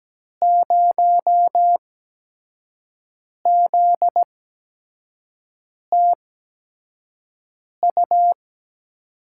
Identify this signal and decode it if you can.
Morse code, heard as a keyed tone, about 17 wpm, spaced as for 5 wpm, 700 Hz. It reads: 0ZTU
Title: USA Call Signs - 6 Characters - 8x Character Spacing - 17wpm